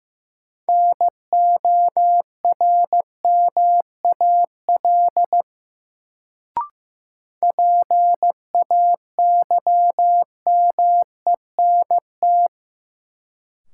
Morse code, heard as a keyed tone, about 15 wpm, 700 Hz. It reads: NORMAL PAYMENT
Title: Sentences from Top 1200 Words - New Word Drill Alt - 15wpm